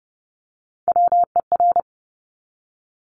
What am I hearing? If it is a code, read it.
Morse code, heard as a keyed tone, about 30 wpm, 700 Hz. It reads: WER
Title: Top Three Letter Combinations in Words 30wpm